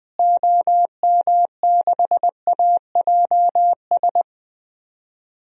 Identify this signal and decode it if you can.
Morse code, heard as a keyed tone, about 20 wpm, 700 Hz. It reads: OM6AJS
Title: DXCC-100 Call Signs - 6 Characters - Repeated 2x - 20wpm